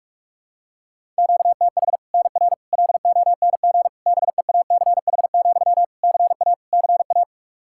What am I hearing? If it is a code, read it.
Morse code, heard as a keyed tone, about 43 wpm, 700 Hz. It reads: QTH NR LONG BEACH, CA CA
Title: K1USN SST QSO Practice 45wpm